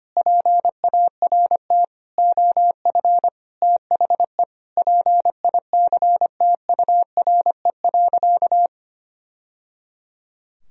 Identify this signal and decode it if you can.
Morse code, heard as a keyed tone, about 25 wpm, 700 Hz. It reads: PART OF THE PICTURE.
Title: Sentences from Top 1000 Words - Review Part 4 - 25wpm